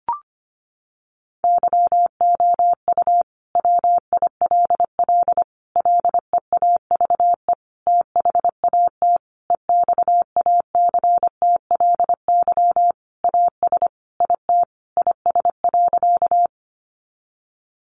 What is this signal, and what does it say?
Morse code, heard as a keyed tone, about 25 wpm, 700 Hz. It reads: YOU WILL LEAVE THAT EXACTLY AS IT IS.
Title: Sentences from Top 900 Words 25wpm